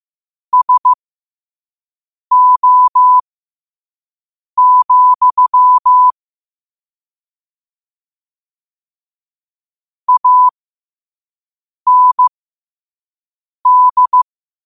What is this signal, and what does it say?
Morse code, heard as a keyed tone, about 15 wpm, spaced as for 5 wpm, 1000 Hz. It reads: SO, AND
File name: 5wpm